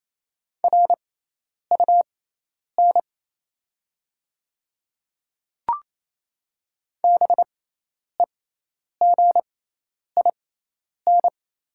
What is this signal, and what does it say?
Morse code, heard as a keyed tone, about 28 wpm, spaced as for 10 wpm, 700 Hz. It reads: RUN BEGIN
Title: 6x Character Spacing - Top 300 Words - 28wpm